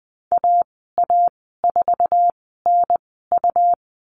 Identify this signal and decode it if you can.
Morse code, heard as a keyed tone, about 20 wpm, spaced as for 15 wpm, 700 Hz. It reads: AA4NU
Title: 128 Lesson 21g - Number 6 - Call Signs 20wpm